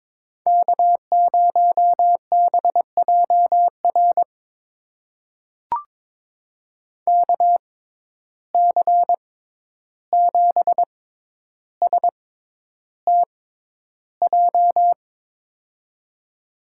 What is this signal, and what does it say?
Morse code, heard as a keyed tone, about 22 wpm, spaced as for 11 wpm, 700 Hz. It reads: K0BJR KC7STJ
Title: USA Call Signs - 6x Character Spacing - 22wpm